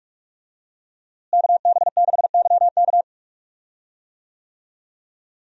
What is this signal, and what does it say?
Morse code, heard as a keyed tone, about 43 wpm, 700 Hz. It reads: KB6YK
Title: USA Call Signs - 5 Characters - Repeated 3x - 45wpm